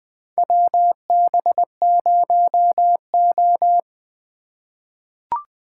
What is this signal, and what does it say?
Morse code, heard as a keyed tone, about 20 wpm, 700 Hz. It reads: WB0O
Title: USA Call Signs - 4 Characters - 3x Character Spacing - 20wpm